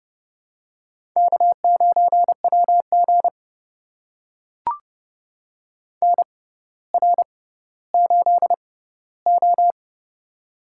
Morse code, heard as a keyed tone, about 30 wpm, spaced as for 15 wpm, 700 Hz. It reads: K9WG NR8O
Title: USA Call Signs - 4 Characters - 6x Character Spacing - 30wpm